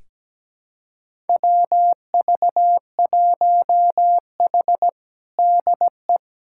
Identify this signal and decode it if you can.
Morse code, heard as a keyed tone, about 17 wpm, 700 Hz. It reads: WV1H DE
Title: USA Callsign DE Callsign - 4 Characters - 17wpm